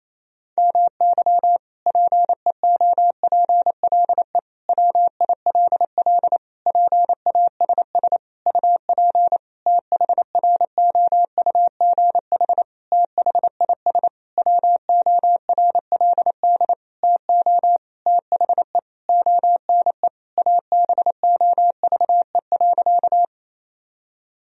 Morse code, heard as a keyed tone, about 28 wpm, 700 Hz. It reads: MY PEOPLE WILL PASS UP THROUGH THIS WORLD TO THE ONE ABOVE.
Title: Sentences from Top 1000 Words - Review Part 2 - 28wpm